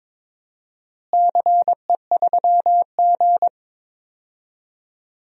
Morse code, heard as a keyed tone, about 22 wpm, 700 Hz. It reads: CE3G
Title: DXCC-100 Call Signs - 4 Characters - Repeated 2x - 22wpm